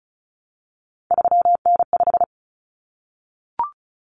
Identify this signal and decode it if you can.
Morse code, heard as a keyed tone, about 34 wpm, 700 Hz. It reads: 3N5
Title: Common Cut Numbers - 3-Digit 35wpm